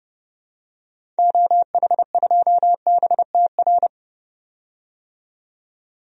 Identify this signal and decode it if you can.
Morse code, heard as a keyed tone, about 30 wpm, 700 Hz. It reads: OH2BTR
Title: DXCC-100 Call Signs - 6 Characters - 30wpm